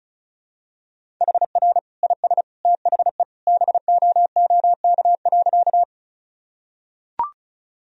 Morse code, heard as a keyed tone, about 36 wpm, 700 Hz. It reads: HR IS THE BOOK.
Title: Sentences from Top 300 Words and Abbreviations 35wpm